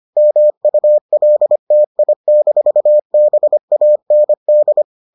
Morse code, heard as a keyed tone, about 25 wpm, 600 Hz. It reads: MULTI-BAND